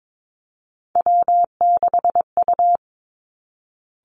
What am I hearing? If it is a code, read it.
Morse code, heard as a keyed tone, about 22 wpm, 700 Hz. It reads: W6U
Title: Three Challenging Letter-Number Combinations 22wpm